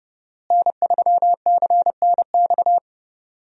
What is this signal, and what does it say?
Morse code, heard as a keyed tone, about 30 wpm, 700 Hz. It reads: N3CNX
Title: USA Call Signs - 5 Characters - Repeated 2x - 30wpm